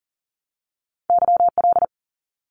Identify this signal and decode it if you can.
Morse code, heard as a keyed tone, about 40 wpm, 700 Hz. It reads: YL
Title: Abbreviations 40wpm